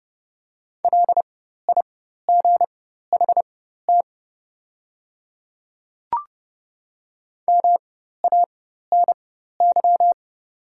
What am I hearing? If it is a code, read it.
Morse code, heard as a keyed tone, about 30 wpm, spaced as for 14 wpm, 700 Hz. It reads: LIGHT MANY
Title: 4x Character Spacing - Top 200 Words - 30wpm